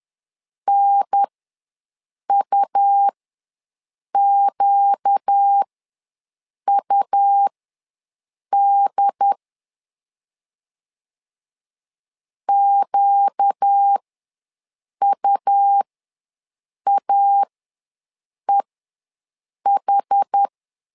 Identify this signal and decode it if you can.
Morse code, heard as a keyed tone, about 10 wpm, spaced as for 6 wpm, 800 Hz. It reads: NUQUD QUAEH
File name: LEZIONI / 4 Q ws 10wpm 2306